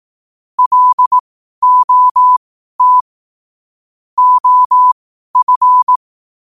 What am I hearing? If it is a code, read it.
Morse code, heard as a keyed tone, about 18 wpm, spaced as for 12 wpm, 1000 Hz. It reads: LOT OF
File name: qotdfiles / 12wpm